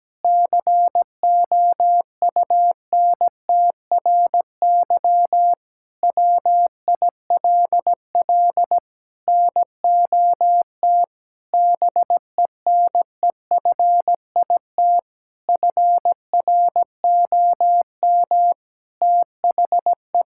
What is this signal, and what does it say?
Morse code, heard as a keyed tone, about 17 wpm, 700 Hz. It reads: COUNTRY WILL NOT BENEFIT FROM THE
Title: Sentences from Top 1000 Words - Review Part 6 - 17wpm